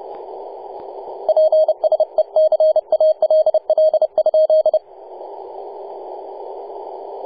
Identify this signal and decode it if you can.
Morse code, heard as a keyed tone, about 30 wpm, 630 Hz, with radio noise. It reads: PSE CALL?
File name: posts / 2015-nov-18-0552z-vk9wa-24-894mhz-cw